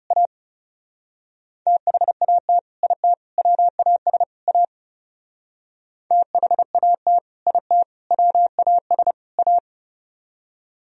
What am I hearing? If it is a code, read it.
Morse code, heard as a keyed tone, about 39 wpm, 700 Hz. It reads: A THAT IT WAS A THAT IT WAS A